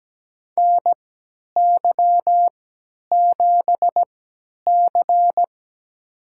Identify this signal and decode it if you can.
Morse code, heard as a keyed tone, about 17 wpm, spaced as for 10 wpm, 700 Hz. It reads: NY7C
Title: USA Call Signs - 4 Characters - 3x Character Spacing - 17wpm